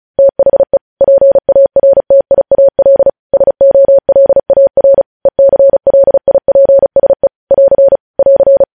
Morse code, heard as a keyed tone, about 36 wpm, 550 Hz. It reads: THE PARTIAL SOLAR ECLIPSE <AR> <AR>
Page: News headlines in Morse code at 35 wpm